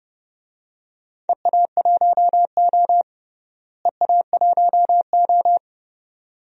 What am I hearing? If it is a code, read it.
Morse code, heard as a keyed tone, about 30 wpm, 700 Hz. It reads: EA1O EA1O
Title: DXCC-100 Call Signs - 4 Characters - Repeated 2x - 30wpm